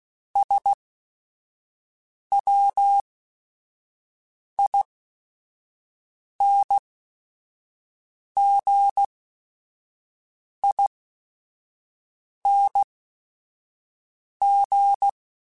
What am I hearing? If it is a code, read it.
Morse code, heard as a keyed tone, about 16 wpm, spaced as for 5 wpm, 800 Hz. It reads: SWINGING